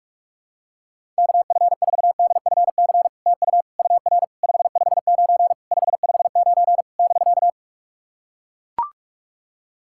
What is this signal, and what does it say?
Morse code, heard as a keyed tone, about 44 wpm, 690 Hz. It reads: KF4DFC TU UR 559 559 <BK>
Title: SOTA QSO Practice 45wpm